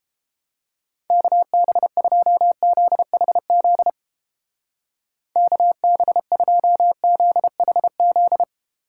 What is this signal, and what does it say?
Morse code, heard as a keyed tone, about 30 wpm, 700 Hz. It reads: KB2ZHZ KB2ZHZ